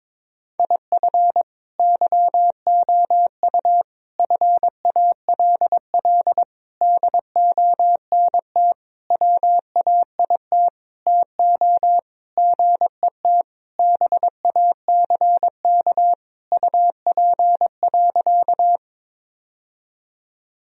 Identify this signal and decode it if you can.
Morse code, heard as a keyed tone, about 21 wpm, 700 Hz. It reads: IF YOU FALL DONT WAIT TO GET BACK UP.
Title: Sentences from Top 1000 Words - Review Part 2 - 22wpm